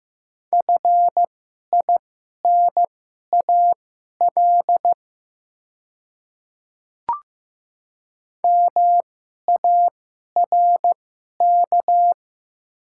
Morse code, heard as a keyed tone, about 15 wpm, spaced as for 11 wpm, 690 Hz. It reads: FINAL MARK
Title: 2x Character Spacing - Top 500 Words - 15wpm